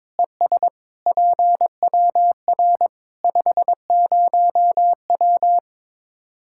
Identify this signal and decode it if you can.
Morse code, heard as a keyed tone, about 22 wpm, 700 Hz. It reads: ES PWR 50W
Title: Practice QSO - Exchange 2 - Info - 22wpm